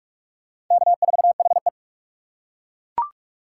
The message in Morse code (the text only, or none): K4HE